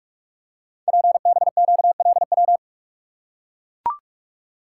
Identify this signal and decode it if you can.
Morse code, heard as a keyed tone, about 44 wpm, 700 Hz. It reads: PBQLW
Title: Code Groups - 5 Letters - 45wpm